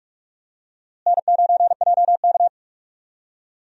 Morse code, heard as a keyed tone, about 43 wpm, 700 Hz. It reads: N9JK
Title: USA Call Signs - 4 Characters - Repeated 2x - 45wpm